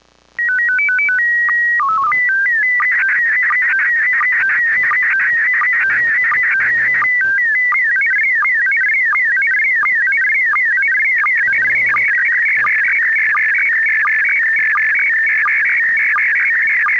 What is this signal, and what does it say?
SSTV (slow-scan television), PD 90.